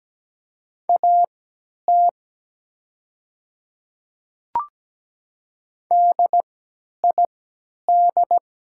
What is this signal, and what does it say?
Morse code, heard as a keyed tone, about 17 wpm, spaced as for 10 wpm, 700 Hz. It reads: AT DID
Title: 3x Character Spacing - Top 100 Words - 17wpm